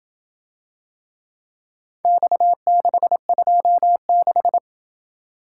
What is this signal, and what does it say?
Morse code, heard as a keyed tone, about 27 wpm, 700 Hz. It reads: X626